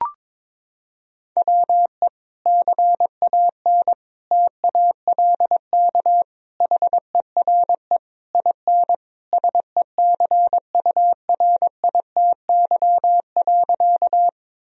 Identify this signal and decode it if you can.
Morse code, heard as a keyed tone, about 22 wpm, 700 Hz. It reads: WE CAN TALK HERE IN SECURITY.
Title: Sentences from Top 1000 Words - Review Part 4 - 22wpm